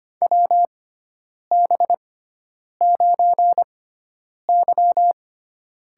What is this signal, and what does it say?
Morse code, heard as a keyed tone, about 25 wpm, spaced as for 9 wpm, 700 Hz. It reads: WB9Y
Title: USA Call Signs - 6 Characters - 6x Character Spacing - 25wpm